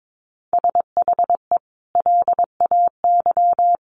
Morse code, heard as a keyed tone, about 22 wpm, 700 Hz. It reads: SHE LAY
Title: Sentences from Top 900 Words 22wpm